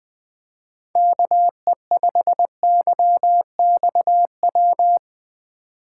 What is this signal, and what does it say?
Morse code, heard as a keyed tone, about 21 wpm, 700 Hz. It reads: KE5YXW